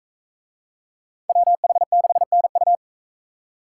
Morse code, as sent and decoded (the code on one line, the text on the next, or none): .-- .... -.... -. ..-
WH6NU